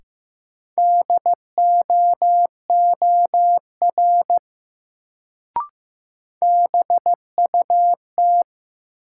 Morse code, heard as a keyed tone, about 15 wpm, 700 Hz. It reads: DOOR BUT
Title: Sets of 2 Words - New Word Drill - 15wpm